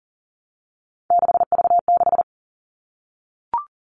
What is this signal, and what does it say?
Morse code, heard as a keyed tone, about 40 wpm, 700 Hz. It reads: 6V6
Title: All Cut Numbers - 3-Digit 40wpm